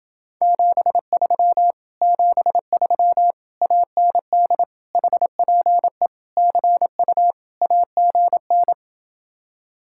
Morse code, heard as a keyed tone, about 29 wpm, 700 Hz. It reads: 73 73 AND HPE CU AGN